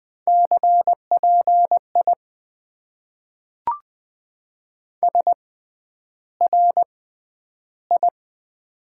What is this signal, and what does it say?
Morse code, heard as a keyed tone, about 20 wpm, spaced as for 10 wpm, 700 Hz. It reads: CPI SRI
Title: Top 90 QSO Elements - 6x Character Spacing - 20wpm